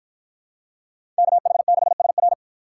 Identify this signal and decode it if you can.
Morse code, heard as a keyed tone, about 53 wpm, 700 Hz. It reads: KH6SR